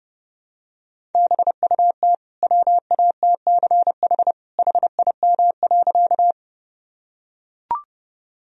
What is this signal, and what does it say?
Morse code, heard as a keyed tone, about 30 wpm, 700 Hz. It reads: BUT WATCH HIM.
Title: Sentences from Top 1000 Words - Review Part 1 - 30wpm